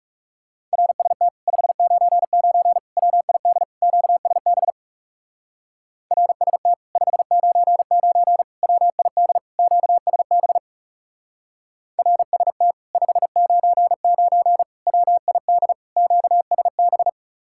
Morse code, heard as a keyed tone, about 40 wpm, 700 Hz. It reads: RST 599 WID QSB RST 599 WID QSB RST 599 WID QSB